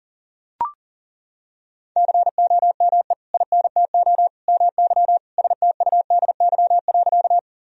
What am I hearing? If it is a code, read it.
Morse code, heard as a keyed tone, about 40 wpm, 700 Hz. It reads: COME INTO MY STUDY.
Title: Sentences from Top 1000 Words - Review Part 1 - 40wpm